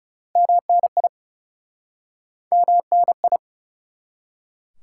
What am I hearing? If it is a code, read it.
Morse code, heard as a keyed tone, about 40 wpm, 700 Hz. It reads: MNI MNI